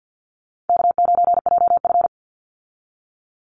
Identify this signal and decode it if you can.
Morse code, heard as a keyed tone, about 48 wpm, 700 Hz. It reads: K9JF
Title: Easy Call Signs Repeated 3x - 50wpm